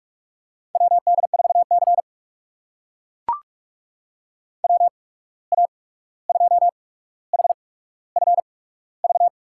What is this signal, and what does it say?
Morse code, heard as a keyed tone, about 44 wpm, spaced as for 17 wpm, 700 Hz. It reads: WD4C WA2HFV
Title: USA Call Signs - 8x Character Spacing - 45wpm